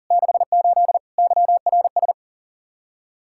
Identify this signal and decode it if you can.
Morse code, heard as a keyed tone, about 50 wpm, 700 Hz. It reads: 68 YRS
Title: Speed-Racing Common QSO Phrases Encore - 40wpm